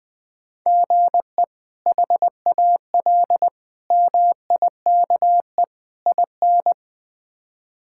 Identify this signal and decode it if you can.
Morse code, heard as a keyed tone, about 20 wpm, 700 Hz. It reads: GE HAL MIKE IN
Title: K1USN SST QSO Practice - 20wpm